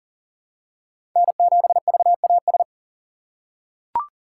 Morse code, heard as a keyed tone, about 40 wpm, 700 Hz. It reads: N7VAS
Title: USA Call Signs - 5 Characters - 3x Character Spacing - 40wpm